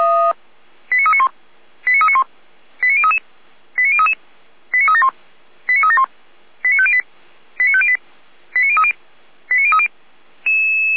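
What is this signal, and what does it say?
Five-tone selective call (ZVEI), heard as quick runs of five tones, tones of about 80 ms. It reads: 89391 89391 89033 89033 89371 89371 89508 89508 89030 89030
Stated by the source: ZVEI five-tone selective call received on SDR.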